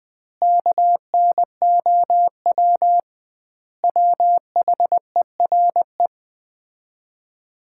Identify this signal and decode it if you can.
Morse code, heard as a keyed tone, about 20 wpm, 700 Hz. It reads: KNOW WHERE
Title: 2x Word-Spacing - Sets of 5 Words - 20wpm